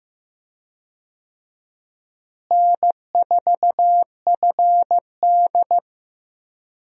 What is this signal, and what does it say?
Morse code, heard as a keyed tone, about 15 wpm, 700 Hz. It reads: N4FD
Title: USA Call Signs - 4 Characters - Repeated 2x - 15wpm